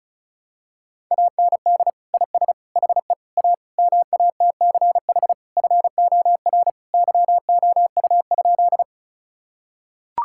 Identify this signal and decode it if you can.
Morse code, heard as a keyed tone, about 34 wpm, 700 Hz. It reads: AND IS HE A MATCH FOR YOU?
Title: Sentences from Top 1600 Words 35wpm